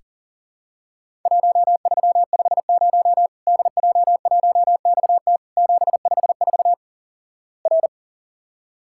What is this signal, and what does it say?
Morse code, heard as a keyed tone, about 37 wpm, 700 Hz. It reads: 1350 DJ1XT 754 R
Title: Worked All Europe DX Contest - QTC Exchange - 40wpm